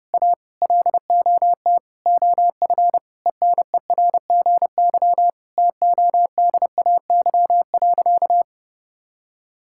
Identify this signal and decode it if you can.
Morse code, heard as a keyed tone, about 32 wpm, 700 Hz. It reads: A LOT OF ENERGY TODAY.
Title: Sentences from Top 1000 Words - Review Part 6 - 30wpm